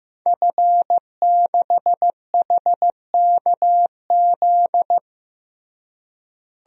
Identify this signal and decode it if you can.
Morse code, heard as a keyed tone, about 16 wpm, 700 Hz. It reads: F6HKZ
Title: DXCC-100 Call Signs - 5 Characters - Repeated 2x - 15wpm